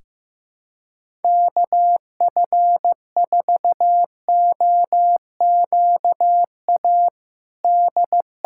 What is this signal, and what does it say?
Morse code, heard as a keyed tone, about 15 wpm, 700 Hz. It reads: KF4OQA D
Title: USA Callsign DE Callsign - 6 Characters - 15wpm